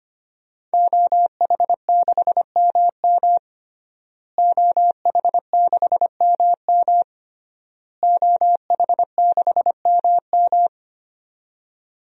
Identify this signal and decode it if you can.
Morse code, heard as a keyed tone, about 25 wpm, 700 Hz. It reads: OH6MM OH6MM OH6MM
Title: DXCC-100 Call Signs - 5 Characters - Repeated 3x - 25wpm